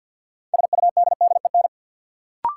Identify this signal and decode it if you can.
Morse code, heard as a keyed tone, about 46 wpm, 700 Hz. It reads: SUDDEN
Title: Sets of 4 Words - New Word Drill - 50wpm